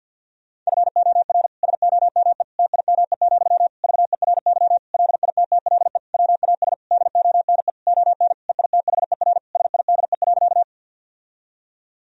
Morse code, heard as a keyed tone, about 50 wpm, 700 Hz. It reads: FOR SOME TIME, VERY LITTLE WAS DONE ON EITHER SIDE.
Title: Sentences from Top 1000 Words - Review Part 4 - 50wpm